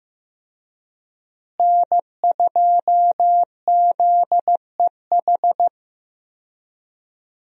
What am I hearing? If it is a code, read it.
Morse code, heard as a keyed tone, about 15 wpm, 700 Hz. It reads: N2ZEH
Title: USA Call Signs - 5 Characters - Repeated 2x - 15wpm